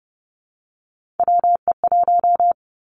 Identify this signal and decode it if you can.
Morse code, heard as a keyed tone, about 30 wpm, 700 Hz. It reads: WE1
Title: Three Challenging Letter-Number Combinations 30wpm